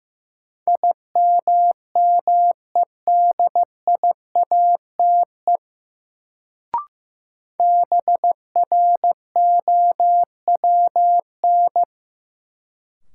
Morse code, heard as a keyed tone, about 15 wpm, 700 Hz. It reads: IMMEDIATE BROWN
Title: Sentences from Top 1200 Words - New Word Drill - 15wpm